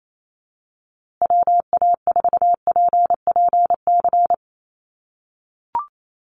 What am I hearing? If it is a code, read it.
Morse code, heard as a keyed tone, about 28 wpm, 700 Hz. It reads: WA4PPC
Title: USA Call Signs Repeated 2x 28wpm